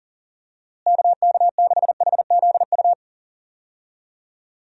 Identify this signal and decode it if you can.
Morse code, heard as a keyed tone, about 40 wpm, 700 Hz. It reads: KK6HZU
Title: USA Call Signs - 6 Characters - Repeated 3x - 40wpm